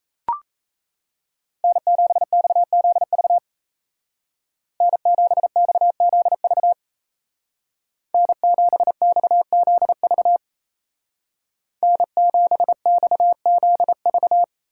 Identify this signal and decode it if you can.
Morse code, heard as a keyed tone, about 34 wpm, 700 Hz. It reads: N7XZV N7XZV N7XZV N7XZV
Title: Speed Racing - USA 1x3 Call Signs 28wpm